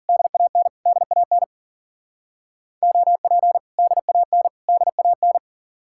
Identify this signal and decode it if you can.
Morse code, heard as a keyed tone, about 50 wpm, 700 Hz. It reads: DAN DAN OP DAN DAN
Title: Speed-Racing Common QSO Phrases - 40wpm